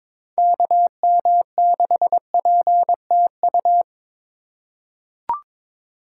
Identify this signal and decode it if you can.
Morse code, heard as a keyed tone, about 22 wpm, 700 Hz. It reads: KM6PTU
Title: USA Call Signs - 6 Characters - 3x Character Spacing - 22wpm